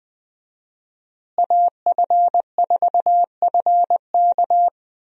Morse code, heard as a keyed tone, about 20 wpm, 700 Hz. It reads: AF4FK
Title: USA Call Signs - 5 Characters - Repeated 3x - 20wpm